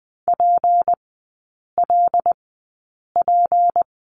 Morse code, heard as a keyed tone, about 20 wpm, spaced as for 8 wpm, 700 Hz. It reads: PLP
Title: 152 Lesson 25e - Letter p - Three Characters 20wpm